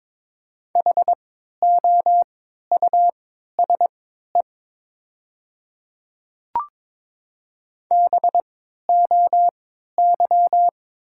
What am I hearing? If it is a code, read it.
Morse code, heard as a keyed tone, about 22 wpm, spaced as for 12 wpm, 700 Hz. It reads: HOUSE BOY
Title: 3x Character Spacing - Top 200 Words - 22wpm